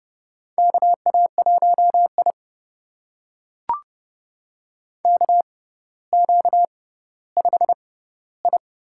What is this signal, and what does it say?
Morse code, heard as a keyed tone, about 32 wpm, spaced as for 16 wpm, 700 Hz. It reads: KA1I KQ5I
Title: USA Call Signs - 4 Characters - 6x Character Spacing - 30wpm